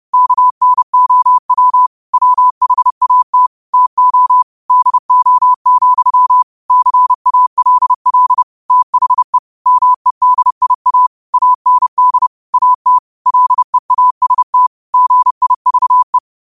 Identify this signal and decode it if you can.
Morse code, heard as a keyed tone, about 30 wpm, 1000 Hz. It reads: KNOW WHAT TO DO, CALL THE MEDIA AND AT LEAST GIVE
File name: qotdfiles / 30wpm